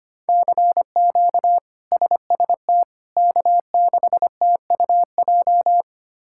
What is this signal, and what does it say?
Morse code, heard as a keyed tone, about 25 wpm, 700 Hz. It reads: CQ SST K6TUJ
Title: K1USN SST QSO Practice - 25wpm